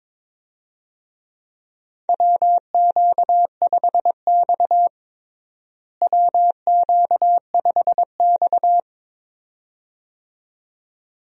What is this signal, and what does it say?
Morse code, heard as a keyed tone, about 22 wpm, 700 Hz. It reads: WQ5X WQ5X
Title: USA Call Signs - 4 Characters - Repeated 2x - 22wpm